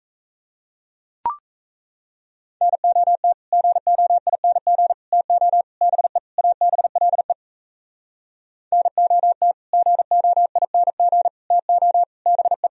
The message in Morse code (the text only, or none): NOT GOING TO BE ABLE NOT GOING TO BE